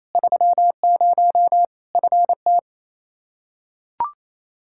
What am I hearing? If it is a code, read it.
Morse code, heard as a keyed tone, about 28 wpm, 700 Hz. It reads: 30 FT
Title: Practice QSO - Exchange 2 - Info - 28wpm